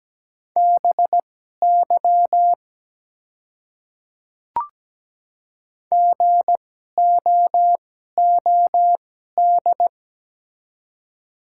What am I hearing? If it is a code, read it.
Morse code, heard as a keyed tone, about 17 wpm, spaced as for 12 wpm, 690 Hz. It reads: BY GOOD
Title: 2x Character Spacing - Top 300 Words - 17wpm